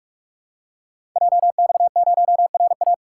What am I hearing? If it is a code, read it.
Morse code, heard as a keyed tone, about 44 wpm, 700 Hz. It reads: JX0RA
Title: Code Groups - 5 Letter:Numbers - 45wpm